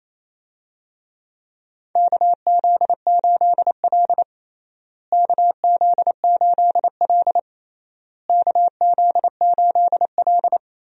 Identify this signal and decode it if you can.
Morse code, heard as a keyed tone, about 28 wpm, 700 Hz. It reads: KZ8L KZ8L KZ8L
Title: USA Call Signs - 4 Characters - Repeated 3x - 28wpm